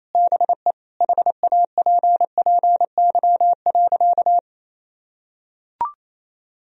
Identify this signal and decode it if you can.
Morse code, heard as a keyed tone, about 28 wpm, 700 Hz. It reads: BE HAPPY.
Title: Sentences from Top 1000 Words - Review Part 1 - 28wpm